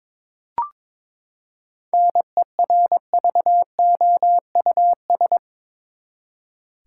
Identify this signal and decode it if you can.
Morse code, heard as a keyed tone, about 22 wpm, 700 Hz. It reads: NERVOUS
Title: Seven Letter Words - 22wpm